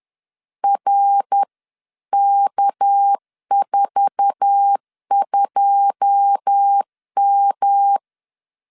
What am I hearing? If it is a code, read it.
Morse code, heard as a keyed tone, about 10 wpm, 800 Hz. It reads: RK42M
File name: LEZIONI / 10 SIGNS / 10A SLASH 10wpm 2808